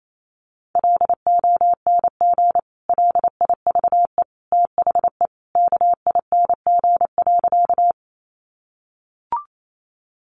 Morse code, heard as a keyed tone, about 28 wpm, 700 Hz. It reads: LONG LIVE THE KING.
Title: Sentences from Top 400 Words 28wpm